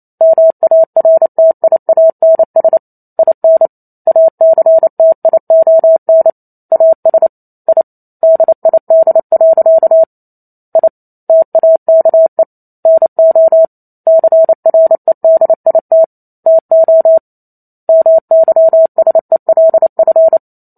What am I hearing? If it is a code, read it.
Morse code, heard as a keyed tone, about 28 wpm, 650 Hz. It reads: MARTIANS IN ACTION AS I DID. I TAKE NO CREDIT TO MYSELF
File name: Chapter 2 - 28 WPM